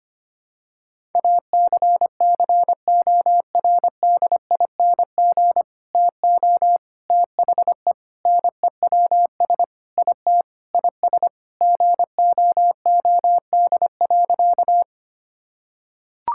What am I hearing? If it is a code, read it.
Morse code, heard as a keyed tone, about 25 wpm, 700 Hz. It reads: ACCORDING TO THE NEWS IT IS GOOD.
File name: practice / 25wpm / Sentences with 3-Word Phrases - 25wpm